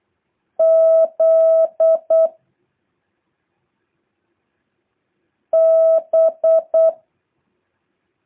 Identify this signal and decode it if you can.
Morse code, heard as a keyed tone, about 8 wpm, 660 Hz. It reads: Z B
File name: jh8cbh / 2023cw